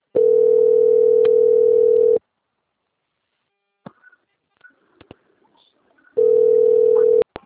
Ringback tone (North America), 440 + 480 Hz.
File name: out- - - - - .18397